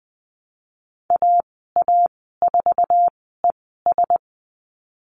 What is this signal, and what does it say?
Morse code, heard as a keyed tone, about 20 wpm, spaced as for 14 wpm, 700 Hz. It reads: AA4ES
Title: 069 Lesson 12g - Letter d - Call Signs 20wpm